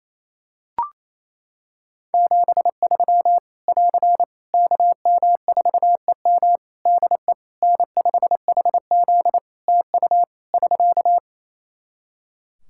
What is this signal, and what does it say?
Morse code, heard as a keyed tone, about 27 wpm, 700 Hz. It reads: 73 <AR> KM4EM DE N5HZ TU <SK>
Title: Practice QSO - Exchange 4 - Ending - 28wpm